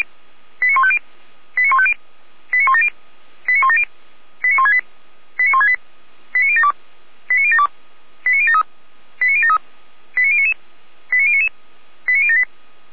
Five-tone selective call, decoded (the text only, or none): 89150 89150 89170 89170 89168 89168 89072 89072 89073 89073 89099 89099 89078